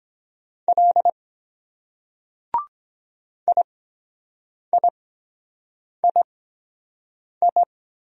Morse code, heard as a keyed tone, about 22 wpm, spaced as for 7 wpm, 700 Hz. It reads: L IIII